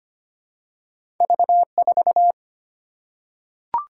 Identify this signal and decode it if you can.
Morse code, heard as a keyed tone, about 25 wpm, 700 Hz. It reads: V4
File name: Two Letter-Number Combinations - 25wpm